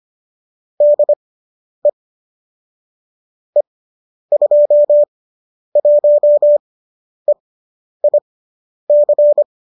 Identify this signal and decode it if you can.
Morse code, heard as a keyed tone, about 25 wpm, spaced as for 10 wpm, 590 Hz. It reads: DE E21EIC
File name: qso005 10